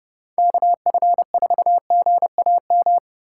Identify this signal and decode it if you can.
Morse code, heard as a keyed tone, about 30 wpm, 700 Hz. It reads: KF4GAM
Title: USA Call Signs - 2x Character Spacing - 30wpm